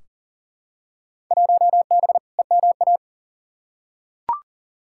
Morse code, heard as a keyed tone, about 40 wpm, 700 Hz. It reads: 1B EMA
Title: Field Day Exchanges - 40wpm